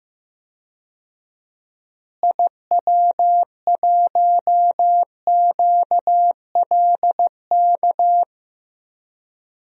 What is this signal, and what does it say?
Morse code, heard as a keyed tone, about 16 wpm, 700 Hz. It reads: IW1QLK
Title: DXCC-100 Call Signs - 6 Characters - Repeated 2x - 15wpm